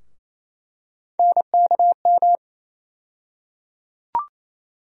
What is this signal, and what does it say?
Morse code, heard as a keyed tone, about 28 wpm, 700 Hz. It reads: NKM
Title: USA Call Sign Postfixes - 3 Letters 28wpm